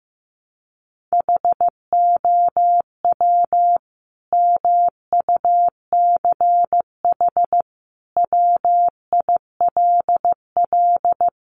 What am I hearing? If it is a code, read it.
Morse code, heard as a keyed tone, about 15 wpm, 700 Hz. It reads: HOW MUCH WILL
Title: Sentences from Top 1400 Words 15wpm